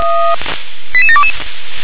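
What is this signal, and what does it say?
Five-tone selective call (ZVEI), heard as quick runs of five tones, tones of about 70 ms. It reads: 88722
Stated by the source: ZVEI five-tone selective call received by SDR.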